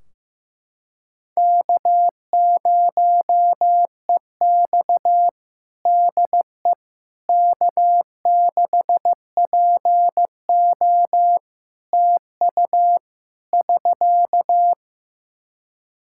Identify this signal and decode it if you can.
Morse code, heard as a keyed tone, about 15 wpm, 700 Hz. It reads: K0EX DE K6PO TU <SK>
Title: Practice QSO - Exchange 4 - Ending - 15wpm